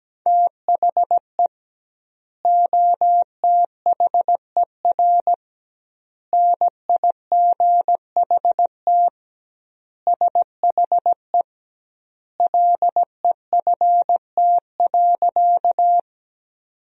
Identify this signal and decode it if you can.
Morse code, heard as a keyed tone, about 17 wpm, 700 Hz. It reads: THE OTHER NIGHT SHE LEFT.
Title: 2x Word-Spacing - Sentences from Top 300 Words - 17wpm